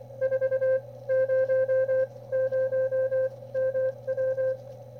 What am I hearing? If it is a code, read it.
Morse code, heard as a keyed tone, about 24 wpm, spaced as for 18 wpm, 520 Hz. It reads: 400MW
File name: qrp / wb0jnr at 400mW with ae5x 02-23-2018